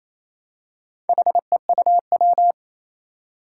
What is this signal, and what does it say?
Morse code, heard as a keyed tone, about 28 wpm, 700 Hz. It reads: HEUW
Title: Four Challenging Letter-Number Combinations 28wpm